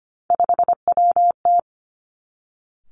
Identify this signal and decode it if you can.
Morse code, heard as a keyed tone, about 25 wpm, 700 Hz. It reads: 5WT
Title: Three Challenging Letter-Number Combinations 25wpm